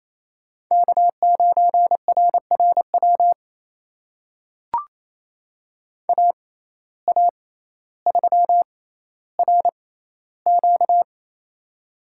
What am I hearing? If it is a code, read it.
Morse code, heard as a keyed tone, about 28 wpm, spaced as for 14 wpm, 700 Hz. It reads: K9RRW AA3RQ
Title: USA Call Signs - 5 Characters - 6x Character Spacing - 28wpm